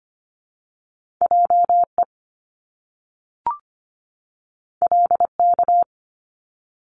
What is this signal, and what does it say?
Morse code, heard as a keyed tone, about 25 wpm, 700 Hz. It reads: JE LK
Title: Top Two Letter Combinations in Words 25wpm